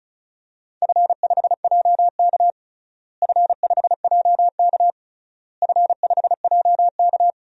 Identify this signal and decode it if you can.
Morse code, heard as a keyed tone, about 36 wpm, 700 Hz. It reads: F5JK F5JK F5JK
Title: DXCC-100 Call Signs - 4 Characters - Repeated 3x - 35wpm